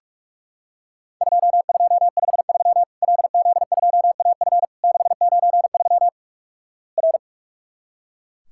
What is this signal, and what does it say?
Morse code, heard as a keyed tone, about 44 wpm, 700 Hz. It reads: 2253 LZ2AF 693 R
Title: Worked All Europe DX Contest - QTC Exchange - 45wpm